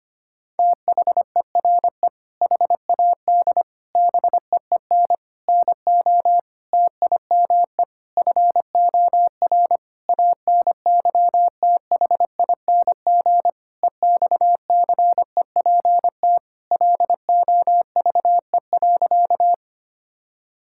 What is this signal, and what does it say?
Morse code, heard as a keyed tone, about 25 wpm, 700 Hz. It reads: THERE HAD BEEN NO TIME FOR ANYTHING EXCEPT LOVE.
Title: Sentences from Top 1600 Words 25wpm